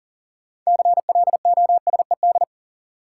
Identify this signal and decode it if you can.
Morse code, heard as a keyed tone, about 40 wpm, 700 Hz. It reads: CLOSED
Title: Sentences from Top 1400 Words - New Word Drill Alt - 40wpm